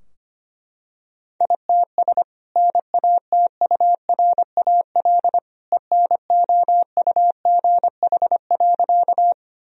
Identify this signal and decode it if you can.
Morse code, heard as a keyed tone, about 25 wpm, 700 Hz. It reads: ITS NATURAL ENOUGH.
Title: Sentences from Top 1000 Words - Review Part 4 - 25wpm